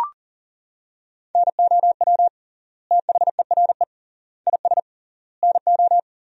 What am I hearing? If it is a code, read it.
Morse code, heard as a keyed tone, about 40 wpm, 700 Hz. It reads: NOW THERE IS NO
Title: 3x Word-Spacing - Sentences from Top 100 Words - 40wpm